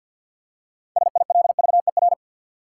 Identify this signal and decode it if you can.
Morse code, heard as a keyed tone, about 49 wpm, 700 Hz. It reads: SILVER
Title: Six Letter Words - 50wpm